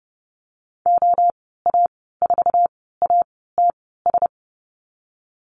Morse code, heard as a keyed tone, about 30 wpm, spaced as for 17 wpm, 700 Hz. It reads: OA4ATS
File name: course / 30:10wpm / 088 Lesson 15g - Number 5 - Call Signs 30wpm